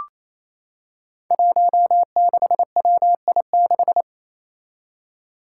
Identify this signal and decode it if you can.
Morse code, heard as a keyed tone, about 28 wpm, 700 Hz. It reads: 16WI6
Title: Five Challenging Letter-Number Combinations 28wpm